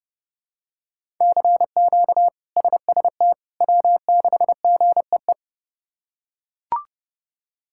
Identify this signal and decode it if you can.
Morse code, heard as a keyed tone, about 32 wpm, 700 Hz. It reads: CQ SST W6GEE
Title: K1USN SST QSO Practice - 30wpm